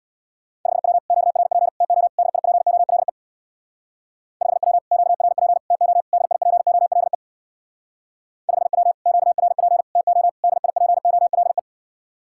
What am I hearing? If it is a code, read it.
Morse code, heard as a keyed tone, about 66 wpm, 700 Hz. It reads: 5W QRP TO DIPOLE 5W QRP TO DIPOLE 5W QRP TO DIPOLE